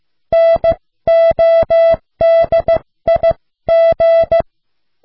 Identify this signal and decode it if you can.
Morse code, heard as a keyed tone, about 15 wpm, 660 Hz. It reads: NODIG